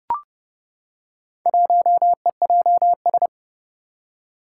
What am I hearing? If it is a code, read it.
Morse code, heard as a keyed tone, about 30 wpm, 700 Hz. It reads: 1EJS
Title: Four Challenging Letter-Number Combinations 30wpm